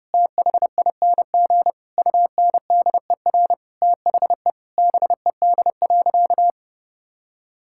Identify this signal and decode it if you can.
Morse code, heard as a keyed tone, about 30 wpm, 700 Hz. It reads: THING UNDER THE BED.
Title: Sentences from Top 1000 Words - Review Part 5 - 30wpm